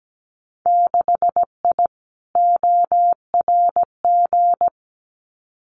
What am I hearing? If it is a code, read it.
Morse code, heard as a keyed tone, about 17 wpm, 700 Hz. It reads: 6I ORG
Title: Winter Field Day Exchanges Encore 17wpm